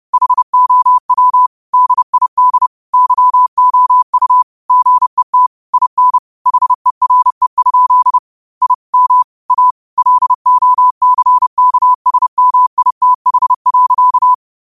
Morse code, heard as a keyed tone, about 30 wpm, 1000 Hz. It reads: HOW DID YOU GET IN HERE? I M A LOCKSMITH.